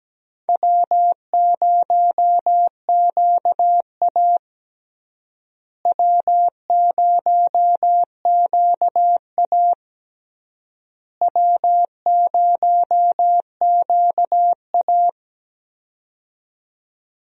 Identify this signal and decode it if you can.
Morse code, heard as a keyed tone, about 17 wpm, 700 Hz. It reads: W0QA W0QA W0QA
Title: USA Call Signs - 4 Characters - Repeated 3x - 17wpm